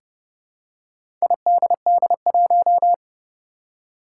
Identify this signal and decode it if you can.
Morse code, heard as a keyed tone, about 30 wpm, 700 Hz. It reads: IDD1
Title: Four Challenging Letter-Number Combinations 30wpm